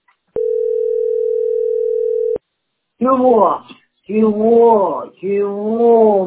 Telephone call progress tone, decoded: ringback tone

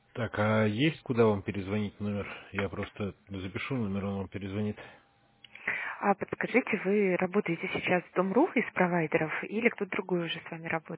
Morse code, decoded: none

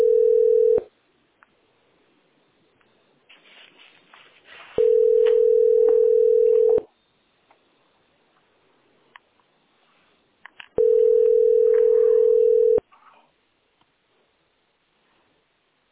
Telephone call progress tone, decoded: ringback tone

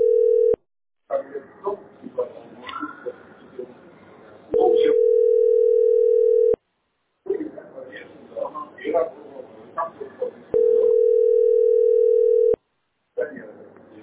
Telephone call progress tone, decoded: ringback tone

